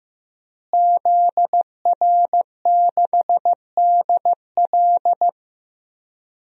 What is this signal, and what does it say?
Morse code, heard as a keyed tone, about 15 wpm, 700 Hz. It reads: ZR6DL
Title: DXCC-100 Call Signs - 5 Characters - Repeated 3x - 15wpm